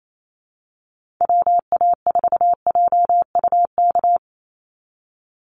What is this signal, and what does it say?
Morse code, heard as a keyed tone, about 28 wpm, 700 Hz. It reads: WA4JUK
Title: Call Signs Repeated 3x - 28wpm